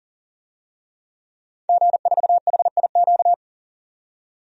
Morse code, heard as a keyed tone, about 40 wpm, 700 Hz. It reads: G4HIQ
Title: DXCC-100 Call Signs - 5 Characters - 40wpm